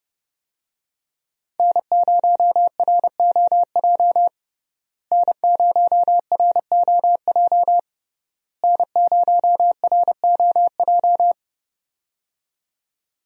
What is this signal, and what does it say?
Morse code, heard as a keyed tone, about 30 wpm, 700 Hz. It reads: N0ROJ N0ROJ N0ROJ
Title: USA Call Signs - 5 Characters - Repeated 3x - 30wpm